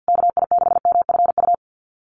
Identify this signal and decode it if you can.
Morse code, heard as a keyed tone, about 50 wpm, 700 Hz. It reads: KI6MFU
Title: USA Call Signs Repeated 3x 50wpm